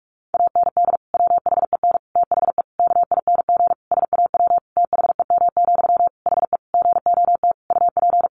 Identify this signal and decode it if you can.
Morse code, heard as a keyed tone, about 44 wpm, 700 Hz. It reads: AND WHEN THE KING SAW THEM, HE GOT UP
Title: Sentences from Top 1100 Words 45wpm